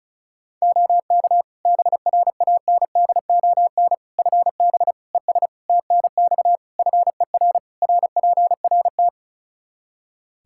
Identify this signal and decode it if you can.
Morse code, heard as a keyed tone, about 36 wpm, 700 Hz. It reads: OK BRANDON FB ES TNX FER RPRT
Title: Practice QSO - Exchange 2 - Info - 35wpm